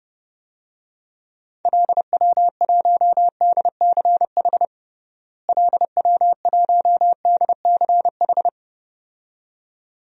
Morse code, heard as a keyed tone, about 30 wpm, 700 Hz. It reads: LW1DCH LW1DCH
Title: DXCC-100 Call Signs - 6 Characters - Repeated 2x - 30wpm